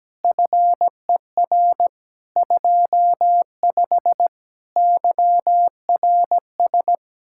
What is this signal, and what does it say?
Morse code, heard as a keyed tone, about 17 wpm, 700 Hz. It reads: FER 25 YRS
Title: Speed-Racing Common QSO Phrases Encore - 17wpm